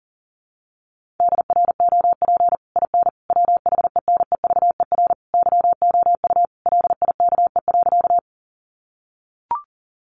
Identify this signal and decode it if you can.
Morse code, heard as a keyed tone, about 40 wpm, 700 Hz. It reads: DROP IN WHENEVER YOU LIKE.
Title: Sentences from Top 1400 Words 40wpm